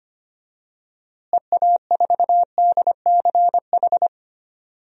Morse code, heard as a keyed tone, about 25 wpm, 700 Hz. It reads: EA4DCH
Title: DXCC-100 Call Signs - 6 Characters - Repeated 2x - 25wpm